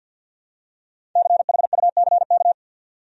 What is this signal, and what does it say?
Morse code, heard as a keyed tone, about 46 wpm, 700 Hz. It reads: CHUCK
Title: USA Common Names - 50wpm